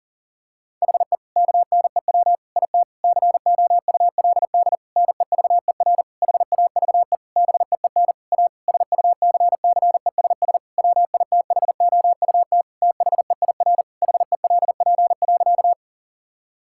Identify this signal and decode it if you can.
Morse code, heard as a keyed tone, about 40 wpm, 700 Hz. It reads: HE KNEW IT COULD NEVER HAVE BEEN A SUCCESS WITHOUT THEIR HELP.
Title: Sentences from Top 1000 Words - Review Part 6 - 40wpm